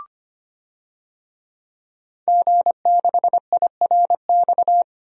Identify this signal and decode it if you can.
Morse code, heard as a keyed tone, about 25 wpm, 700 Hz. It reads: G6IRX
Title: DXCC-100 Call Signs - 5 Characters - 25wpm